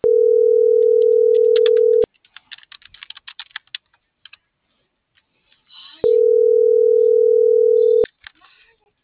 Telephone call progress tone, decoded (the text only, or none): ringback tone